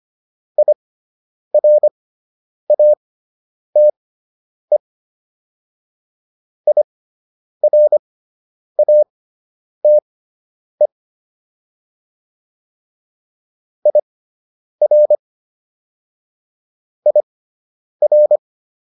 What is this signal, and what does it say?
Morse code, heard as a keyed tone, about 25 wpm, spaced as for 9 wpm, 600 Hz. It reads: IRATE IRATE IR IR